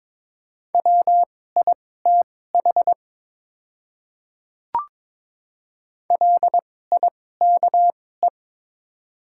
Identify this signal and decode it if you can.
Morse code, heard as a keyed tone, about 22 wpm, spaced as for 16 wpm, 690 Hz. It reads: WITH LIKE
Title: 2x Character Spacing - Top 100 Words - 22wpm